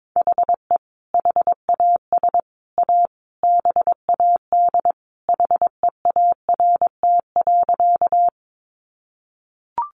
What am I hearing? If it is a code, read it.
Morse code, heard as a keyed tone, about 22 wpm, 700 Hz. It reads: HE HAS A BAD HEART.
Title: Sentences from Top 1200 Words 22wpm